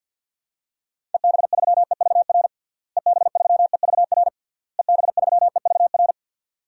Morse code, heard as a keyed tone, about 51 wpm, 700 Hz. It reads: EB3EVR EB3EVR EB3EVR
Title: DXCC-100 Call Signs - 6 Characters - Repeated 3x - 50wpm